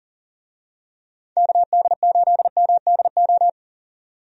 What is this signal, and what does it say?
Morse code, heard as a keyed tone, about 40 wpm, 700 Hz. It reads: KD8MDO
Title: USA Call Signs - 6 Characters - 40wpm